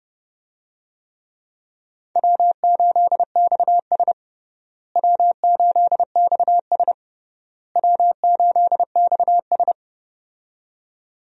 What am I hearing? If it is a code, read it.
Morse code, heard as a keyed tone, about 31 wpm, 700 Hz. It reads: W8XS W8XS W8XS
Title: USA Call Signs - 4 Characters - Repeated 3x - 30wpm